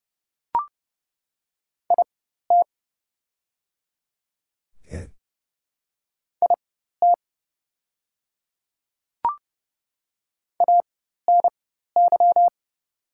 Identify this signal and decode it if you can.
Morse code, heard as a keyed tone, about 30 wpm, spaced as for 14 wpm, 700 Hz. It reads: IT IT ANY
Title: 4x Character Spacing - Top 300 Words - 30wpm